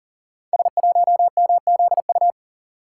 Morse code, heard as a keyed tone, about 39 wpm, 700 Hz. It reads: S1MZU
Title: Code Groups - 5 Letter:Numbers - 40wpm